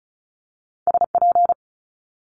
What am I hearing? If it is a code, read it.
Morse code, heard as a keyed tone, about 36 wpm, 700 Hz. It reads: SP
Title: Multi-letter Phonemes Quick Fire 35wpm